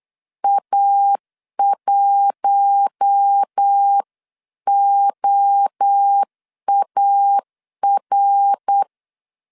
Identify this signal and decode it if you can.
Morse code, heard as a keyed tone, about 8 wpm, 800 Hz. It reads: A1OAR
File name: 8wpm